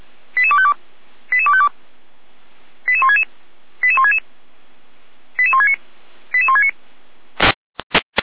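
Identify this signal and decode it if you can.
Five-tone selective call (ZVEI), heard as quick runs of five tones, tones of about 70 ms. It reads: 88252 88252 88166 88166 88169 88169
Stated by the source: ZVEI five-tone selective call received via SDR, from the Feuerwehr Orlamünde audio server.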